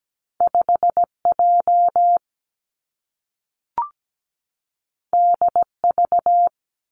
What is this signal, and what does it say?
Morse code, heard as a keyed tone, about 17 wpm, 700 Hz. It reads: 5J DV
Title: Two Challenging Letter-Number Combinations 17wpm